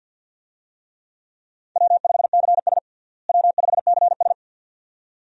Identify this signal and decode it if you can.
Morse code, heard as a keyed tone, about 48 wpm, 700 Hz. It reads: W5CS W5CS
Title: USA Call Signs - 4 Characters - Repeated 2x - 50wpm